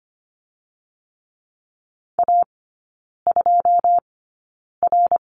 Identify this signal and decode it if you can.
Morse code, heard as a keyed tone, about 25 wpm, spaced as for 9 wpm, 700 Hz. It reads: A2R
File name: course / 25:10wpm / 093 Lesson 16e - Letter c - Three Characters 25wpm